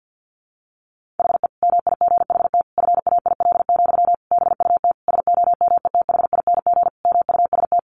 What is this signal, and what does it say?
Morse code, heard as a keyed tone, about 50 wpm, 700 Hz. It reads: HE MIGHT FAIL, BUT SOMETHING MUST
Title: Sentences from Top 1100 Words 50wpm